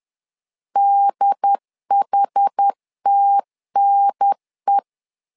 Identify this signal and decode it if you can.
Morse code, heard as a keyed tone, about 10 wpm, 800 Hz. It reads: DHTNE
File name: LEZIONI / 5 LBVC / 5 L 10wpm 2259